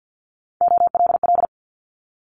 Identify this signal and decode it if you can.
Morse code, heard as a keyed tone, about 50 wpm, 700 Hz. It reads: OLL
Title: Multi-letter Phonemes Quick Fire 50wpm